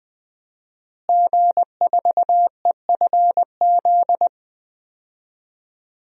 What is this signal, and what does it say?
Morse code, heard as a keyed tone, about 20 wpm, 700 Hz. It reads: G4EFZ
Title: DXCC-100 Call Signs - 5 Characters - Repeated 2x - 20wpm